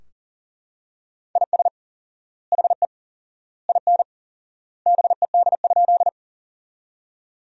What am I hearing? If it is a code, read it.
Morse code, heard as a keyed tone, about 37 wpm, 700 Hz. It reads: IS HE IN BED?
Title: 4x Word-Spacing - Sentences from Top 500 Words - 40wpm